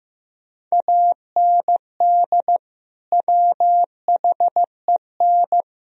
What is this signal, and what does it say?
Morse code, heard as a keyed tone, about 15 wpm, 700 Hz. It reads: AND WHEN
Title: Sentences from Top 1000 Words - Review Part 2 - 15wpm